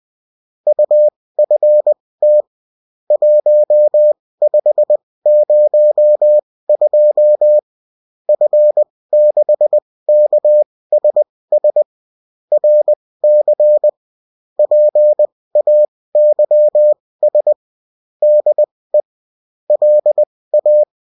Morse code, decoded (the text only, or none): UFT 1502 F6KSS RC PAYS DE LA